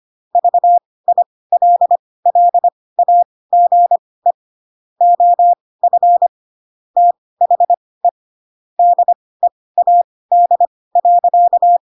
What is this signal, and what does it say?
Morse code, heard as a keyed tone, about 25 wpm, spaced as for 18 wpm, 700 Hz. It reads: VILLAGE OF THE DEAD.
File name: Pinocchio / Pinocchio ch6 12 25 ik1hge